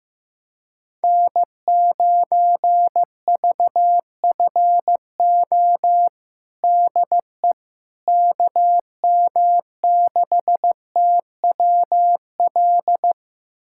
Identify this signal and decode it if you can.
Morse code, heard as a keyed tone, about 15 wpm, 700 Hz. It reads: N9VFO DE KM6TWL
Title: USA Callsign DE Callsign 15wpm